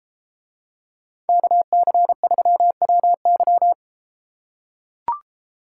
Morse code, heard as a keyed tone, about 33 wpm, 700 Hz. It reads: KC3WY